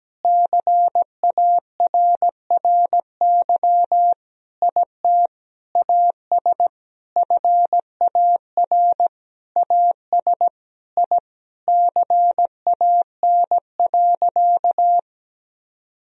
Morse code, heard as a keyed tone, about 17 wpm, 700 Hz. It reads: CARRY IT AS FAR AS I CAN.
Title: Sentences from Top 1000 Words - Review Part 1 - 17wpm